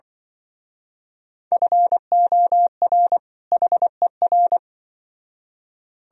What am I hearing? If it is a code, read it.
Morse code, heard as a keyed tone, about 24 wpm, 700 Hz. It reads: FOR HER
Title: Speed-Racing - Sets of 2 Words 20wpm